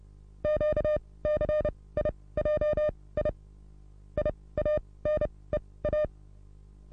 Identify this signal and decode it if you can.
Morse code, heard as a keyed tone, about 31 wpm, spaced as for 20 wpm, 600 Hz. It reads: QCIJI IANEA
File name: repos / Morsekurs / DJ4UF Morsen27